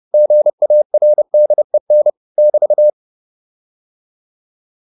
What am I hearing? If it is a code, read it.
Morse code, heard as a keyed tone, about 30 wpm, 600 Hz. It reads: GARDEN <BT>
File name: QSO410 30